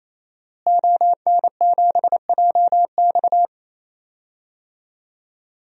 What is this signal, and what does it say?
Morse code, heard as a keyed tone, about 27 wpm, 700 Hz. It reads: ON7JX
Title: DXCC-100 Call Signs - 5 Characters - Repeated 3x - 28wpm